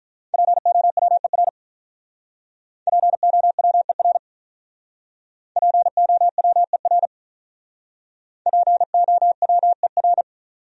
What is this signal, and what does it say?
Morse code, heard as a keyed tone, about 47 wpm, 700 Hz. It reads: POWER POWER POWER POWER